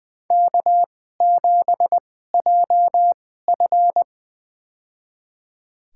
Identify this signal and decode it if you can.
Morse code, heard as a keyed tone, about 20 wpm, spaced as for 14 wpm, 700 Hz. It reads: K7JF
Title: USA Call Signs - 4 Characters - 2x Character Spacing - 20wpm